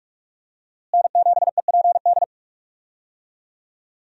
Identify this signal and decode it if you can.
Morse code, heard as a keyed tone, about 46 wpm, 700 Hz. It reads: N7EPD
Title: USA Call Signs - 5 Characters - Repeated 2x - 45wpm